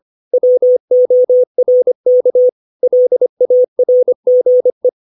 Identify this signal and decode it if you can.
Morse code, heard as a keyed tone, about 25 wpm, 500 Hz. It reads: WORK LARGE
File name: 632 words / 632 Words 25WPM